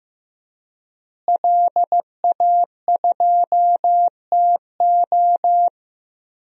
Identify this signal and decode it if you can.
Morse code, heard as a keyed tone, about 15 wpm, 700 Hz. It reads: LA2TO
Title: DXCC-100 Call Signs - 5 Characters - Repeated 2x - 15wpm